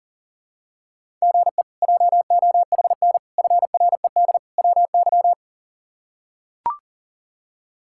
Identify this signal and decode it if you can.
Morse code, heard as a keyed tone, about 40 wpm, 700 Hz. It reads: GE JOHN FRED WY
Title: K1USN SST QSO Practice - 40wpm